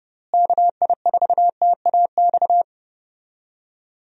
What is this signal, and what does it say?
Morse code, heard as a keyed tone, about 30 wpm, 700 Hz. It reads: KI4TAX
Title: USA Call Signs - 6 Characters - Repeated 3x - 30wpm